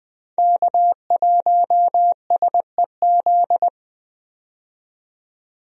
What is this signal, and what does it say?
Morse code, heard as a keyed tone, about 21 wpm, 700 Hz. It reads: K1SEZ
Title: USA Call Signs - 5 Characters - 20wpm